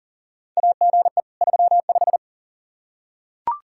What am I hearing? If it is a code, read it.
Morse code, heard as a keyed tone, about 40 wpm, 690 Hz. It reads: AGE 35
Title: Practice QSO - Exchange 3 - More Info - 40wpm